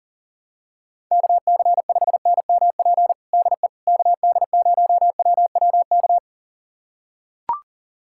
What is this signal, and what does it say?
Morse code, heard as a keyed tone, about 40 wpm, 700 Hz. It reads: KC5NMP DE KD0WWK
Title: USA Callsign DE Callsign 40wpm